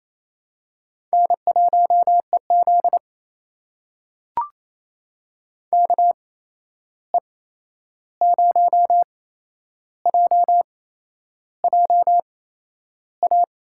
Morse code, heard as a keyed tone, about 27 wpm, spaced as for 10 wpm, 700 Hz. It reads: N1EZ KE0JJA
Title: USA Call Signs - 8x Character Spacing - 28wpm